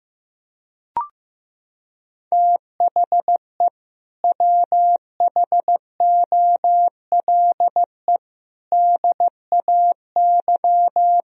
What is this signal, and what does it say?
Morse code, heard as a keyed tone, about 15 wpm, 700 Hz. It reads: THE WHOLE DAY
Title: Sentences from Top 1000 Words - Review Part 3 - 15wpm